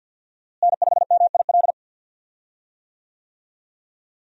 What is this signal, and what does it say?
Morse code, heard as a keyed tone, about 46 wpm, 700 Hz. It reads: N5MIL
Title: USA Call Signs - 5 Characters - Repeated 3x - 50wpm